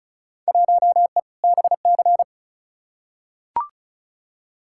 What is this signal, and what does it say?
Morse code, heard as a keyed tone, about 36 wpm, 700 Hz. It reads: 1E BC
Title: Field Day Exchanges - 35wpm